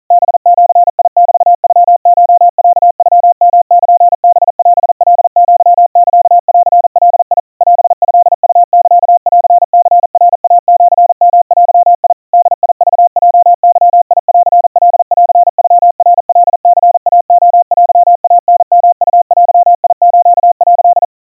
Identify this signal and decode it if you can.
Morse code (Japanese Wabun code), heard as a keyed tone, about 40 wpm, 700 Hz. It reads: ハジメノコロノヨルハオカアサンガ オトウサンニナイシヨデ ボクヲゲンカンノナカニイレテイタヨウデス」
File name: kawa / Morse sample / kamyu 150 cpm 700 Hz Crispy